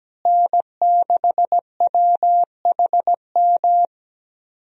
Morse code, heard as a keyed tone, about 17 wpm, 700 Hz. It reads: N6WHM
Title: USA Call Signs - 5 Characters - Repeated 2x - 17wpm